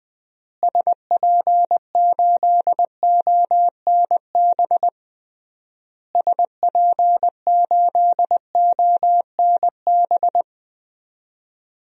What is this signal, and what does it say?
Morse code, heard as a keyed tone, about 20 wpm, 700 Hz. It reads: SP8ONB SP8ONB
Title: DXCC-100 Call Signs - 6 Characters - Repeated 3x - 20wpm